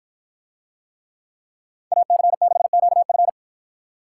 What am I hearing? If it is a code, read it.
Morse code, heard as a keyed tone, about 50 wpm, 700 Hz. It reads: AC6QF